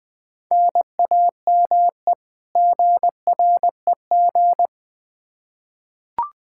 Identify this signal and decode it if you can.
Morse code, heard as a keyed tone, about 20 wpm, 700 Hz. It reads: NAME GREG
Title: Practice QSO - Exchange 1 - Beginning - 20wpm